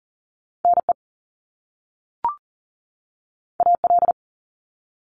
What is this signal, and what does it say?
Morse code, heard as a keyed tone, about 38 wpm, 700 Hz. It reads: NE AL
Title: US States 40wpm